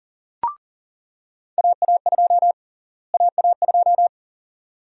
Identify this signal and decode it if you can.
Morse code, heard as a keyed tone, about 40 wpm, 700 Hz. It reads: AA2 AA2
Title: USA Call Sign Prefixes - Repeated 2x - 40wpm